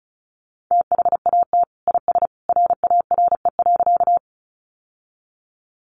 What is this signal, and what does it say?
Morse code, heard as a keyed tone, about 34 wpm, 700 Hz. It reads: THAT IS RARE.
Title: Sentences from Top 1100 Words 35wpm